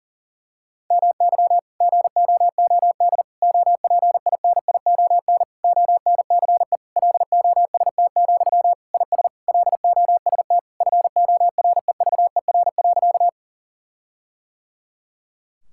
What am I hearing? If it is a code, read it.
Morse code, heard as a keyed tone, about 37 wpm, 700 Hz. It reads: MY GOOD OPINION ONCE LOST, IS LOST FOREVER.
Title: Sentences from Top 1600 Words 40wpm